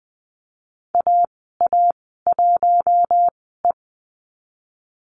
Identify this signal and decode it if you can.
Morse code, heard as a keyed tone, about 20 wpm, spaced as for 14 wpm, 700 Hz. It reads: AA1E
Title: 042 Lesson 8g - Number 1 - Call Signs 20wpm